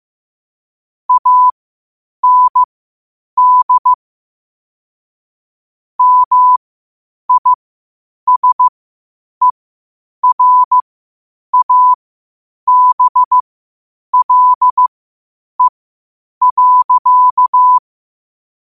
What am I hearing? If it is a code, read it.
Morse code, heard as a keyed tone, about 15 wpm, spaced as for 8 wpm, 1000 Hz. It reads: AND MISERABLE.
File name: qotdfiles / 8wpm